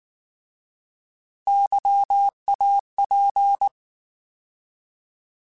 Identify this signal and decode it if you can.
Morse code, heard as a keyed tone, about 19 wpm, 800 Hz. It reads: YAP
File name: Alphabetical / Alphabetical mp3s / Ys @18